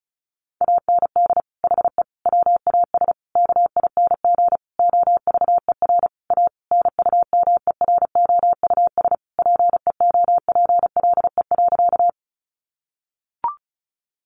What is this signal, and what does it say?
Morse code, heard as a keyed tone, about 34 wpm, 700 Hz. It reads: AND HE WAS KING OVER A NUMEROUS PEOPLE.
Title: Sentences from Top 1400 Words 35wpm